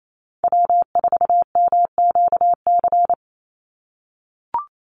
Morse code, heard as a keyed tone, about 28 wpm, 700 Hz. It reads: W4MQC
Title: Call Signs - 28wpm